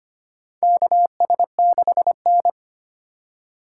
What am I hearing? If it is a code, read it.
Morse code, heard as a keyed tone, about 25 wpm, 700 Hz. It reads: KS6N
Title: USA Call Signs - 4 Characters - 2x Character Spacing - 25wpm